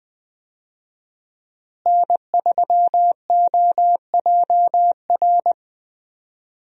Morse code, heard as a keyed tone, about 20 wpm, 700 Hz. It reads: N3OJR
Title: USA Call Signs - 5 Characters - Repeated 2x - 20wpm